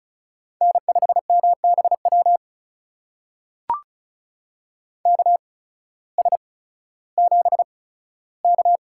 Morse code, heard as a keyed tone, about 34 wpm, spaced as for 16 wpm, 700 Hz. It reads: N5MBW KS7K
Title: USA Call Signs - 8x Character Spacing - 35wpm